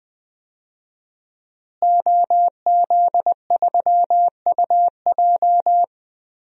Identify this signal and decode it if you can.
Morse code, heard as a keyed tone, about 20 wpm, 700 Hz. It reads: OZ3UJ
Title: DXCC-100 Call Signs - 5 Characters - Repeated 2x - 20wpm